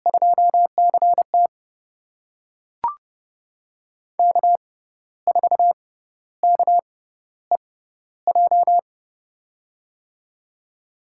Morse code, heard as a keyed tone, about 30 wpm, spaced as for 13 wpm, 700 Hz. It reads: 2CT K4KEJ
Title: USA Call Signs - 5 Characters - 6x Character Spacing - 30wpm